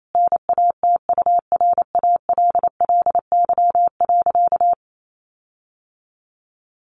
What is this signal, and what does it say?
Morse code, heard as a keyed tone, about 28 wpm, 700 Hz. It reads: NATURALLY.
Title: Sentences from Top 1300 Words 28wpm